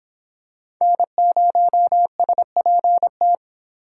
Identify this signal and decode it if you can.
Morse code, heard as a keyed tone, about 26 wpm, 700 Hz. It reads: N0SPT